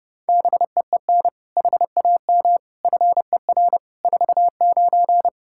Morse code, heard as a keyed tone, about 30 wpm, 700 Hz. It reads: BEEN HAM FER 49
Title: Practice QSO - Exchange 3 - More Info - 30wpm